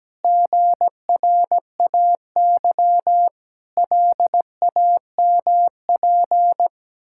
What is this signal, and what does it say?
Morse code, heard as a keyed tone, about 17 wpm, 700 Hz. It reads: GRAY LAMP
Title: Pangram Challenge v2 - 17wpm